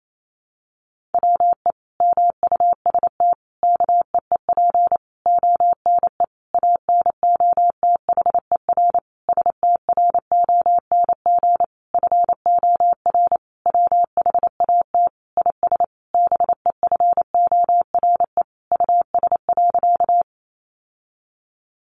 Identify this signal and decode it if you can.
Morse code, heard as a keyed tone, about 28 wpm, 700 Hz. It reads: WE MUST KEEP ONE ANOTHER STRONG FOR WHAT IS BEFORE US.
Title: Sentences from Top 600 Words 28wpm